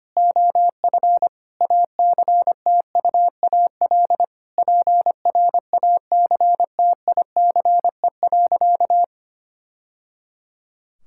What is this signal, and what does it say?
Morse code, heard as a keyed tone, about 25 wpm, 700 Hz. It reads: OF ACTUAL PRACTICE.
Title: Sentences from Top 1000 Words - Review Part 5 - 25wpm